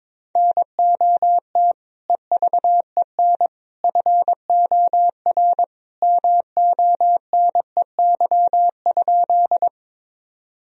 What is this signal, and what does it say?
Morse code, heard as a keyed tone, about 22 wpm, 700 Hz. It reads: NOT EVEN FOR MONEY?
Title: Sentences from Top 1000 Words - Review Part 1 - 22wpm